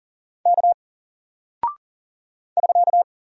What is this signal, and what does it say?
Morse code, heard as a keyed tone, about 40 wpm, 700 Hz. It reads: K <SK>
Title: Single Character (Letter:Number:Punctuation:Prosign) Alt 40wpm